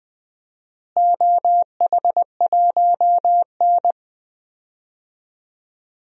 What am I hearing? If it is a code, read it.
Morse code, heard as a keyed tone, about 20 wpm, 700 Hz. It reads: OH1N
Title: DXCC-100 Call Signs - 4 Characters - Repeated 3x - 20wpm